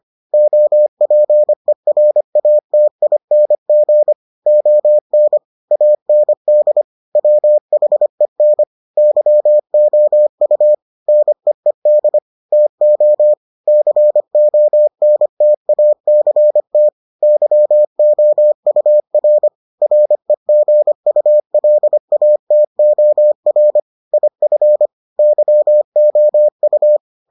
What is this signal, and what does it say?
Morse code, heard as a keyed tone, about 25 wpm, 600 Hz. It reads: OPERATING ON AND WHEN YOU NEED TO CONTACT YOUR REGULATOR IF YOU